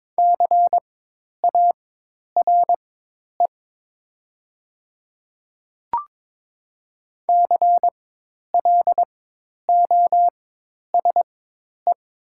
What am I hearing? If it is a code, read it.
Morse code, heard as a keyed tone, about 22 wpm, spaced as for 10 wpm, 700 Hz. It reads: CARE CLOSE
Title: 4x Character Spacing - Top 300 Words - 22wpm